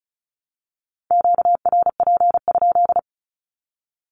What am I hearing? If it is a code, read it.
Morse code, heard as a keyed tone, about 34 wpm, 700 Hz. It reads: QRP?
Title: Q-Codes 35wpm